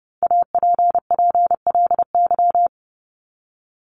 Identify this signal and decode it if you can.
Morse code, heard as a keyed tone, about 30 wpm, 700 Hz. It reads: APPLY
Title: ED Suffix - Top 100 Words with Base Words 30wpm